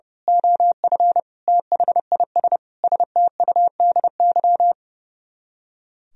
Morse code, heard as a keyed tone, about 28 wpm, 700 Hz. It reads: OF THIS STUDY
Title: Speed-Racing - Sets of 5 Words 30wpm